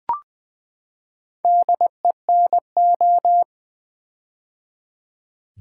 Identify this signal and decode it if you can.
Morse code, heard as a keyed tone, about 20 wpm, 700 Hz. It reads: DENO
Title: Build A Word - 4 Letters At A Time - 8-Letter Words - 20wpm